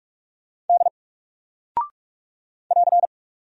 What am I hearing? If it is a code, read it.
Morse code, heard as a keyed tone, about 44 wpm, 700 Hz. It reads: D <AR>
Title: Single Character (Letter:Number:Punctuation:Prosign) Alt 45wpm